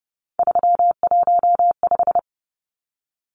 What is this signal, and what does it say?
Morse code, heard as a keyed tone, about 30 wpm, 700 Hz. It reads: 315
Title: Numbers - 3-Digit 30wpm